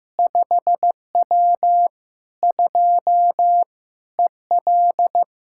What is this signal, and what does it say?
Morse code, heard as a keyed tone, about 15 wpm, 700 Hz. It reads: 5W 2 EL
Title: K1USN SST QSO Practice 15wpm